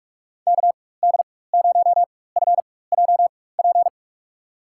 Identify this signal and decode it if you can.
Morse code, heard as a keyed tone, about 46 wpm, spaced as for 21 wpm, 700 Hz. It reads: KD0FJP
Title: USA Call Signs - 6 Characters - 4x Character Spacing - 45wpm